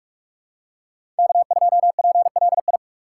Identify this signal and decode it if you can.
Morse code, heard as a keyed tone, about 42 wpm, 700 Hz. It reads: K2PLI
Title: USA Call Signs - 3x Character Spacing - 45wpm